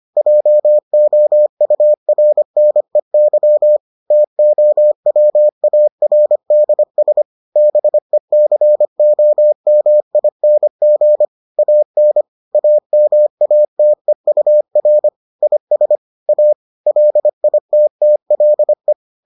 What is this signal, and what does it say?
Morse code, heard as a keyed tone, about 25 wpm, 600 Hz. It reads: JOURNEY TOWARDS BECOMING AN AMATEUR IS A LITTLE